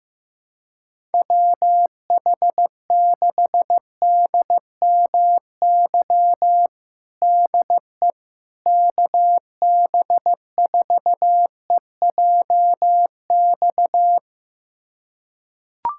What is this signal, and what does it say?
Morse code, heard as a keyed tone, about 16 wpm, 700 Hz. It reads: WH6DMY DE KB4EJX
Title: USA Callsign DE Callsign - 6 Characters - 15wpm